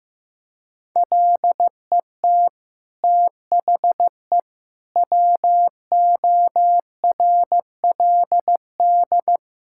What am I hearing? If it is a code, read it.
Morse code, heard as a keyed tone, about 16 wpm, 700 Hz. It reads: LET THE WORLD
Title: Sentences from Top 1600 Words 15wpm